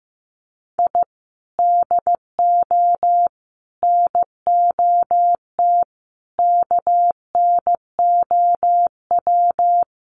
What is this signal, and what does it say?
Morse code, heard as a keyed tone, about 14 wpm, 700 Hz. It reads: I DO NOT KNOW
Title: Sets of 5 Words 15wpm